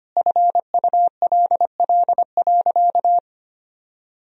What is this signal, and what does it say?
Morse code, heard as a keyed tone, about 25 wpm, 700 Hz. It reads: FULL.
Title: Sentences from Top 1000 Words - Review Part 3 - 25wpm